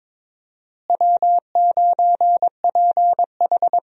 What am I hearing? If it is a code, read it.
Morse code, heard as a keyed tone, about 22 wpm, 700 Hz. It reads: W9PH
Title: USA Call Signs - 4 Characters - Repeated 3x - 22wpm